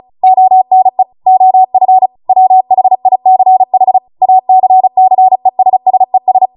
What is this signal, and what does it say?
Morse code, heard as a keyed tone, about 36 wpm, 750 Hz. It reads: ONE OF WHICH ACCESSES